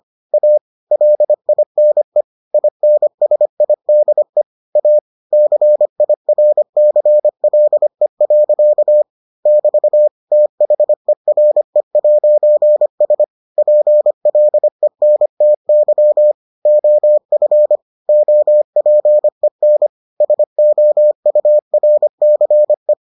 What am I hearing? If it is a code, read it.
Morse code, heard as a keyed tone, about 25 wpm, 600 Hz. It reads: A LINE INSIDE A CIRCLE. <BT> THERE'S PLENTY OF OPEN SOURCE